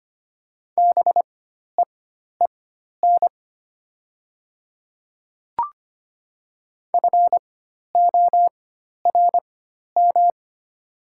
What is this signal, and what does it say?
Morse code, heard as a keyed tone, about 25 wpm, spaced as for 12 wpm, 700 Hz. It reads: BEEN FORM
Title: 4x Character Spacing - Top 300 Words - 25wpm